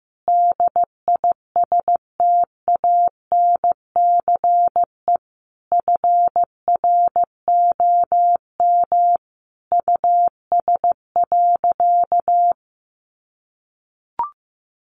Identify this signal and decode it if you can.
Morse code, heard as a keyed tone, about 15 wpm, 700 Hz. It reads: DISTANCE FROM US.
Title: Sentences from Top 1200 Words 15wpm